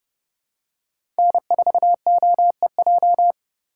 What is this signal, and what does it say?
Morse code, heard as a keyed tone, about 31 wpm, 700 Hz. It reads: N4OEJ
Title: USA Call Signs - 5 Characters - 2x Character Spacing - 30wpm